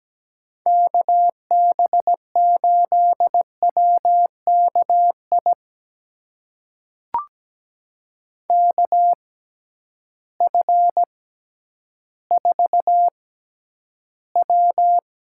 Morse code, heard as a keyed tone, about 17 wpm, spaced as for 10 wpm, 700 Hz. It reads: KB8WKI KF4W
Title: USA Call Signs - 6 Characters - 6x Character Spacing - 17wpm